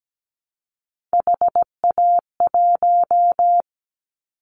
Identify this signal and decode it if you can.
Morse code, heard as a keyed tone, about 17 wpm, 700 Hz. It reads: HA1
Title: Callsign Prefixes for Top 50 DXCC - 17wpm